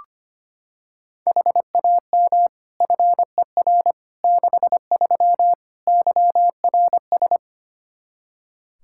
Morse code, heard as a keyed tone, about 25 wpm, 700 Hz. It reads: HAM FER 63 YRS
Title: Practice QSO - Exchange 3 - More Info - 25wpm